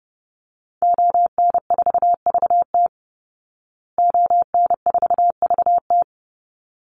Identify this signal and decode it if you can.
Morse code, heard as a keyed tone, about 30 wpm, 700 Hz. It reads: ON4VT ON4VT
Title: Call Signs Repeated 3x - 30wpm